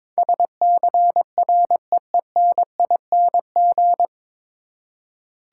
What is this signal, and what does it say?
Morse code, heard as a keyed tone, about 21 wpm, 700 Hz. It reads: SCREENING
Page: Morse Code Ninja practice 22wpm